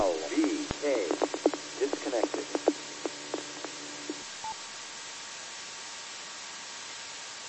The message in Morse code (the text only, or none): none